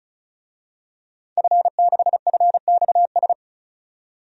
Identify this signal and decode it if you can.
Morse code, heard as a keyed tone, about 36 wpm, 700 Hz. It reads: F6FXS
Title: DXCC-100 Call Signs - 5 Characters - 35wpm